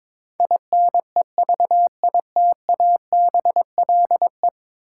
Morse code, heard as a keyed tone, about 22 wpm, 700 Hz. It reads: INEVITABLE
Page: morse-code-ninja-practice-22wpm